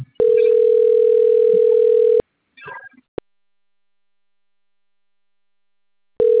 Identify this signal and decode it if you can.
Ringback tone (North America), 440 + 480 Hz.